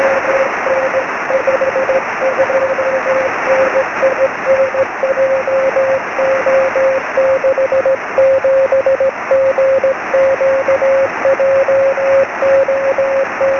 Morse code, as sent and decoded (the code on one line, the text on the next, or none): none